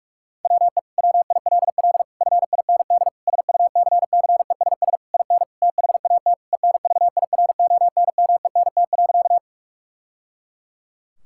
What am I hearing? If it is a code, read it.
Morse code, heard as a keyed tone, about 46 wpm, 700 Hz. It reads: WE WILL FIND SUCCESS IN THAT ENVIRONMENT.
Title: Sentences from Top 1000 Words - Review Part 5 - 45wpm